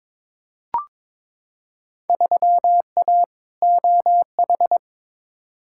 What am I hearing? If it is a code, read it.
Morse code, heard as a keyed tone, about 22 wpm, 700 Hz. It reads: 3A OH
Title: Field Day Exchanges - 22wpm